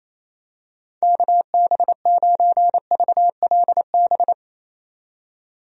Morse code, heard as a keyed tone, about 27 wpm, 700 Hz. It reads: KB9VLB
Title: USA Call Signs - 6 Characters - 28wpm